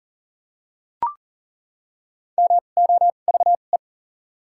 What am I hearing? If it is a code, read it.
Morse code, heard as a keyed tone, about 40 wpm, spaced as for 29 wpm, 700 Hz. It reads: MOVE
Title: 2x Character Spacing - Top 200 Words - 40wpm